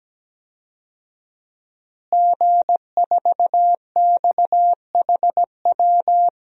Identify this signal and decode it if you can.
Morse code, heard as a keyed tone, about 17 wpm, 700 Hz. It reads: G4XHW
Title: DXCC-100 Call Signs - 5 Characters - Repeated 2x - 17wpm